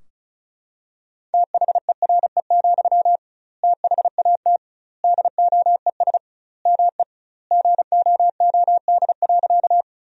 Morse code, heard as a keyed tone, about 36 wpm, 700 Hz. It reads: THERE, THAT DOES ME GOOD.
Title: 2x Word-Spacing - Sentences from Top 200 Words - 35wpm